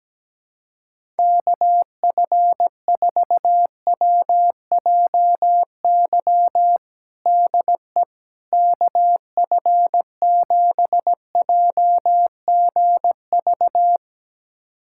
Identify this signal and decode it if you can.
Morse code, heard as a keyed tone, about 17 wpm, 700 Hz. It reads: KF4WJY DE KF7JGV
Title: USA Callsign DE Callsign 17wpm